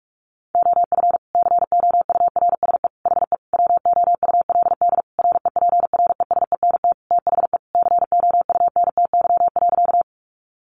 Morse code, heard as a keyed tone, about 44 wpm, 700 Hz. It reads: OF COURSE HE WOULD REPRESENT THE COUNTY.
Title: Sentences from Top 1200 Words 45wpm